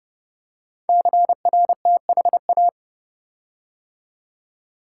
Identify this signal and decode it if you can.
Morse code, heard as a keyed tone, about 30 wpm, 700 Hz. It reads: CRTHA
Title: Code Groups - 5 Letters - 30wpm